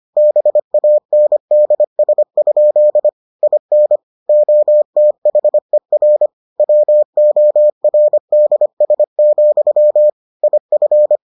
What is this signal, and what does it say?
Morse code, heard as a keyed tone, about 25 wpm, 600 Hz. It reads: BANDS? IN OTHER WORDS, IF